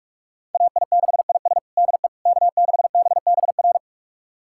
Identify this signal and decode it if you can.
Morse code, heard as a keyed tone, about 41 wpm, 700 Hz. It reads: AI6IS DE K6BBR
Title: USA Callsign DE Callsign - 5 Characters - 45wpm